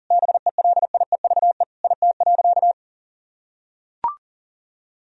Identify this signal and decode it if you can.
Morse code, heard as a keyed tone, about 40 wpm, 700 Hz. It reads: BELIEVE IT.
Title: Sentences from Top 1000 Words - Review Part 1 - 40wpm